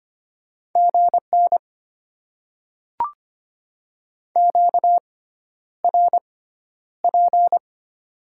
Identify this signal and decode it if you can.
Morse code, heard as a keyed tone, about 25 wpm, spaced as for 11 wpm, 700 Hz. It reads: GN QRP
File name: Top 90 QSO Elements - 6x Character Spacing - 25wpm